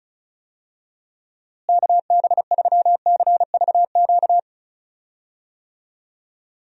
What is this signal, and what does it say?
Morse code, heard as a keyed tone, about 36 wpm, 700 Hz. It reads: KB3CVQ
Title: USA Call Signs - 6 Characters - Repeated 3x - 35wpm